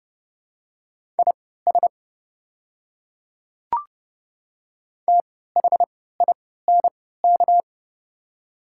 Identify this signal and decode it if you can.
Morse code, heard as a keyed tone, about 30 wpm, spaced as for 17 wpm, 700 Hz. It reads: IS THINK
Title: 3x Character Spacing - Top 200 Words - 30wpm